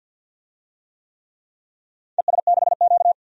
Morse code, heard as a keyed tone, about 51 wpm, 700 Hz. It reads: ES6Q
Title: DXCC-100 Call Signs - 4 Characters - 50wpm